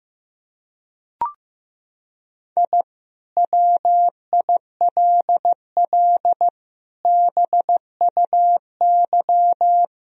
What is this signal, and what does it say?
Morse code, heard as a keyed tone, about 15 wpm, 700 Hz. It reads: I WILL BUY
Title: Sentences from Top 1000 Words - Review Part 4 - 15wpm